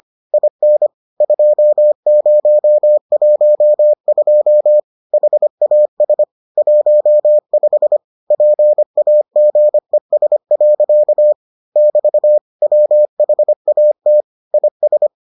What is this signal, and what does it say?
Morse code, heard as a keyed tone, about 25 wpm, 600 Hz. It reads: IN 2012 HAS 15 PAGES. <BT> WHAT IS